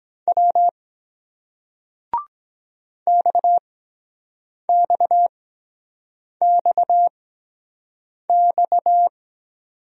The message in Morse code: W X X X X

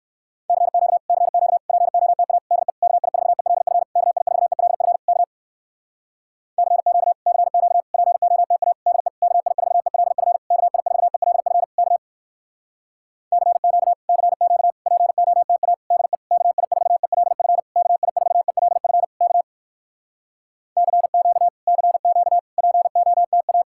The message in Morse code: CQ CQ POTA DE KI4ELF KI4ELF K CQ CQ POTA DE KI4ELF KI4ELF K CQ CQ POTA DE KI4ELF KI4ELF K CQ CQ POTA

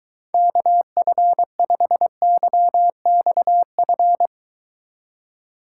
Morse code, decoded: KF5YXF

